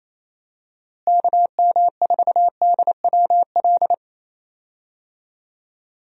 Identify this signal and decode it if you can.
Morse code, heard as a keyed tone, about 28 wpm, 700 Hz. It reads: KM4DWL
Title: USA Call Signs - 6 Characters - Repeated 2x - 28wpm